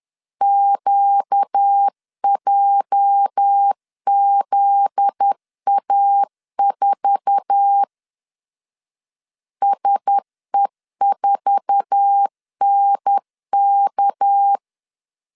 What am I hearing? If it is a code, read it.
Morse code, heard as a keyed tone, about 10 wpm, 800 Hz. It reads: QJZA4 SE4NK
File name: LEZIONI / 8 YJ45 / 8 YJ4 10wpm 2633